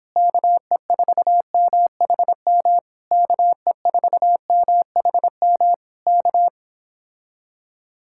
Morse code, heard as a keyed tone, about 26 wpm, 700 Hz. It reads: KE4MHM KE4MHM K